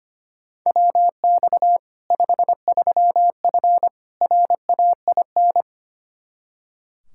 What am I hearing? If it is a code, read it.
Morse code, heard as a keyed tone, about 25 wpm, 700 Hz. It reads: WX 53F RAIN
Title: Practice QSO - Exchange 2 - Info - 25wpm